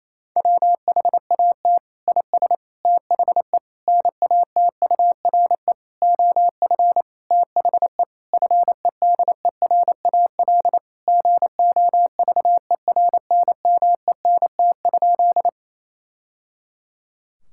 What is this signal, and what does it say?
Morse code, heard as a keyed tone, about 28 wpm, 700 Hz. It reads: WHAT IS THE NATURE OF THE FEDERAL GOVERNMENT?
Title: Sentences from Top 1000 Words - Review Part 5 - 28wpm